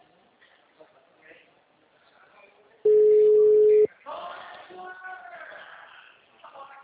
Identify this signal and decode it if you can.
Ringback tone (Europe, 425 Hz), with speech.